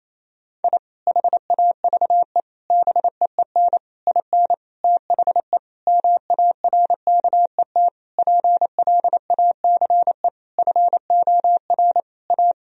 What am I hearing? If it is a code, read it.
Morse code, heard as a keyed tone, about 27 wpm, 700 Hz. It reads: I HAVE BEEN IN THE MARKET PLACE FOR A
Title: Sentences from Top 1000 Words - Review Part 3 - 28wpm